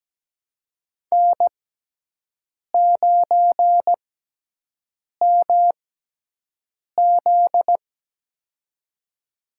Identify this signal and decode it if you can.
Morse code, heard as a keyed tone, about 17 wpm, spaced as for 6 wpm, 700 Hz. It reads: N9MZ
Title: USA Call Signs - 4 Characters - 6x Character Spacing - 17wpm